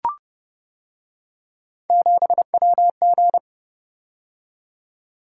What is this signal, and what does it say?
Morse code, heard as a keyed tone, about 29 wpm, 700 Hz. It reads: 7WG